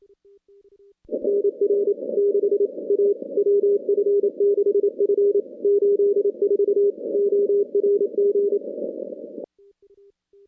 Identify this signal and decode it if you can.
Morse code, heard as a keyed tone, about 25 wpm, 410 Hz. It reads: NR 6 A WF6F 84 ORG
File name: contests / CW-SS-k5tr- -2200